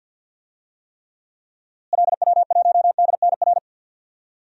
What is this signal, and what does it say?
Morse code, heard as a keyed tone, about 53 wpm, 700 Hz. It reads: LW1DNR
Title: DXCC-100 Call Signs - 6 Characters - Repeated 2x - 50wpm